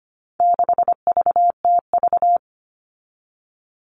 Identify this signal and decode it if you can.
Morse code, heard as a keyed tone, about 25 wpm, 700 Hz. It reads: 6VTV
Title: All Cut Numbers - 4-Digit 25wpm